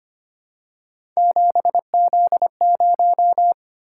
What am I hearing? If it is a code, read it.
Morse code, heard as a keyed tone, about 25 wpm, 700 Hz. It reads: 7Z0
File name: Three Letter-Number Combinations - 25wpm